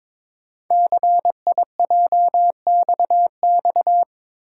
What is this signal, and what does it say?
Morse code, heard as a keyed tone, about 23 wpm, 700 Hz. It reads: CIJXX
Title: Code Groups - 5 Letters - 22wpm